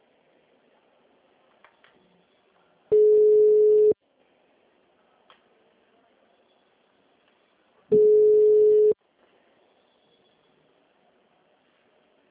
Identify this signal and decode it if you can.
Ringback tone (Europe, 425 Hz).